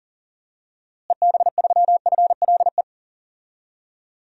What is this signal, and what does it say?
Morse code, heard as a keyed tone, about 37 wpm, 700 Hz. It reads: EB3FLE
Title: DXCC-100 Call Signs - 6 Characters - Repeated 2x - 40wpm